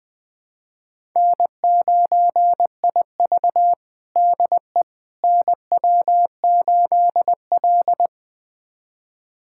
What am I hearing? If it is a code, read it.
Morse code, heard as a keyed tone, about 21 wpm, 700 Hz. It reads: N9IV DE NW8L
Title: USA Callsign DE Callsign - 4 Characters - 20wpm